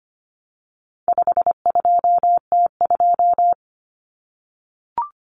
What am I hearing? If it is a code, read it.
Morse code, heard as a keyed tone, about 25 wpm, 700 Hz. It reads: 52T2
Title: Common Cut Numbers - 4-Digit 25wpm